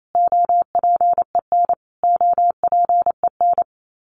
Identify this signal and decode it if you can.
Morse code, heard as a keyed tone, about 27 wpm, 700 Hz. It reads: OPEN OPEN
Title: Top 300 Words Repeated 28wpm